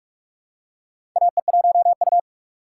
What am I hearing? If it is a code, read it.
Morse code, heard as a keyed tone, about 42 wpm, 700 Hz. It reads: AE1U
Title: USA Call Signs - 4 Characters - 3x Character Spacing - 45wpm